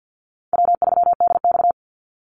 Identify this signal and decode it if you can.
Morse code, heard as a keyed tone, about 50 wpm, 700 Hz. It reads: W3DX
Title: NA SOTA Activator Callsigns 50wpm